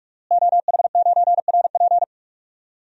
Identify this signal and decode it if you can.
Morse code, heard as a keyed tone, about 46 wpm, 700 Hz. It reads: OH9RP
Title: DXCC-100 Call Signs - 5 Characters - Repeated 2x - 45wpm